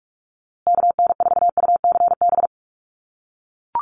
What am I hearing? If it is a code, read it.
Morse code, heard as a keyed tone, about 44 wpm, 690 Hz. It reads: KN4UCB
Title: USA Call Signs Repeated 2x 45wpm